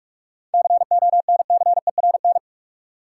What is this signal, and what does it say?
Morse code, heard as a keyed tone, about 46 wpm, 700 Hz. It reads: CONCERN
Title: Seven Letter Words - 45wpm